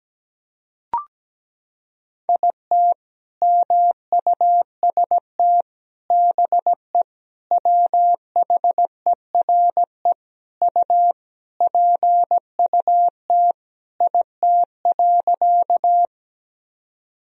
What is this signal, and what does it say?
Morse code, heard as a keyed tone, about 17 wpm, 700 Hz. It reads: IT MUST BE WHERE U PUT IT.
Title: Sentences from Top 200 Words and Abbreviations 17wpm